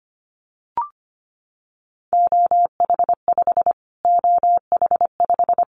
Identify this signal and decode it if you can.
Morse code, heard as a keyed tone, about 25 wpm, 700 Hz. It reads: OH5 OH5
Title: Callsign Prefixes for Top 25 DXCC - Repeated 3x - 25wpm